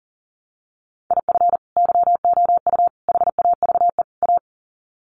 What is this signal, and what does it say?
Morse code, heard as a keyed tone, about 40 wpm, 700 Hz. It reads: IF YOU HAVE A
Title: Sets of 4 Words 40wpm